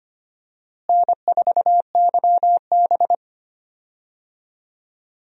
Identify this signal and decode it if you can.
Morse code, heard as a keyed tone, about 25 wpm, 700 Hz. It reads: N4YB
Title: USA Call Signs - 4 Characters - Repeated 3x - 25wpm